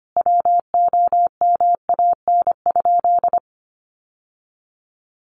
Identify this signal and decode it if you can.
Morse code, heard as a keyed tone, about 25 wpm, 700 Hz. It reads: WOMAN?
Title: Sentences from Top 1100 Words 25wpm